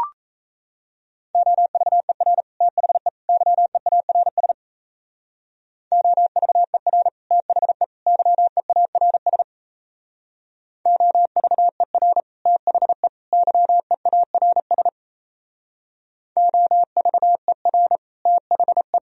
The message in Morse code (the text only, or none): OVER THE YEARS OVER THE YEARS OVER THE YEARS OVER THE